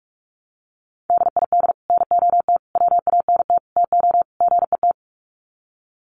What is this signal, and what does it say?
Morse code, heard as a keyed tone, about 44 wpm, 700 Hz. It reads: DID NOT WANT TO GET
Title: Sets of 5 Words Encore2 45wpm